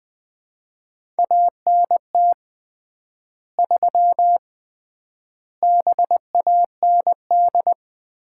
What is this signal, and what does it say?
Morse code, heard as a keyed tone, about 20 wpm, 700 Hz. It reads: ANT 3 BAND
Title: Common QSO Phrases - 3x Word-Spacing - 20wpm